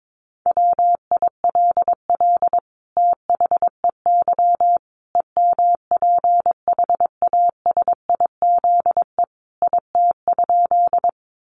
Morse code, heard as a keyed tone, about 22 wpm, 700 Hz. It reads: WILL THEY EMPHASIZE IT?
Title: Sentences from Top 1500 Words 22wpm